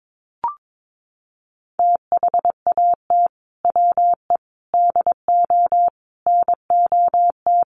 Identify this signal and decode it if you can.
Morse code, heard as a keyed tone, about 22 wpm, 700 Hz. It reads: THAT WE DO NOT
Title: Sets of 4 Words 22wpm